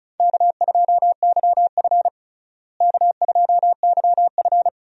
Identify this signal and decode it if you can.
Morse code, heard as a keyed tone, about 36 wpm, 700 Hz. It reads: K2YF K2YF
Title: USA Call Signs - 4 Characters - Repeated 2x - 35wpm